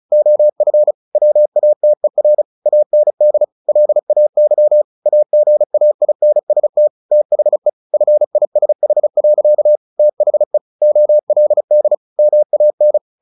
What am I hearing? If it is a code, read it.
Morse code, heard as a keyed tone, about 34 wpm, 600 Hz. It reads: OF WATER AND LAY AGAINST THE FISH. THE OLD MAN
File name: the old man and the sea 2 35wpm